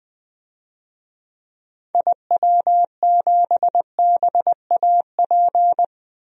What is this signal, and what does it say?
Morse code, heard as a keyed tone, about 20 wpm, 700 Hz. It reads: IW7BAP
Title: DXCC-100 Call Signs - 6 Characters - Repeated 3x - 20wpm